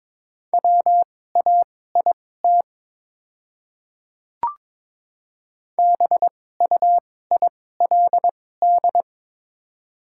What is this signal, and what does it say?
Morse code, heard as a keyed tone, about 22 wpm, spaced as for 16 wpm, 700 Hz. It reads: WAIT BUILD
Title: 2x Character Spacing - Top 500 Words - 22wpm